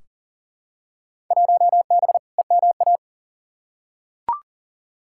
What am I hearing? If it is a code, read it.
Morse code, heard as a keyed tone, about 40 wpm, 700 Hz. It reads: 1B EMA
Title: Field Day Exchanges - 40wpm